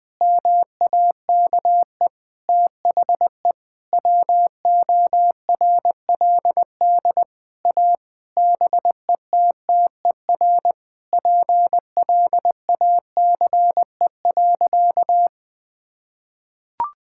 Morse code, heard as a keyed tone, about 21 wpm, 700 Hz. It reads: MAKE THE WORLD A BETTER PLACE.
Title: Sentences with 7-Word Phrases - 20wpm